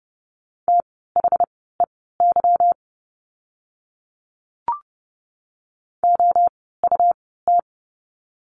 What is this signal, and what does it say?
Morse code, heard as a keyed tone, about 30 wpm, spaced as for 17 wpm, 700 Hz. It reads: THEY OUT
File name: course / 30:10wpm / 237 Lesson 38f - Letter q - Words 30wpm